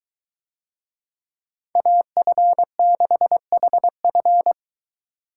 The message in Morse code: AF6HF

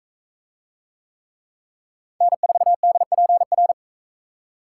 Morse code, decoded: N4DPR

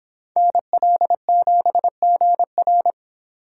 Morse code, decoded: NL7GR

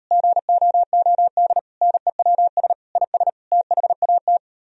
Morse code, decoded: GOOD NEWS IS THAT

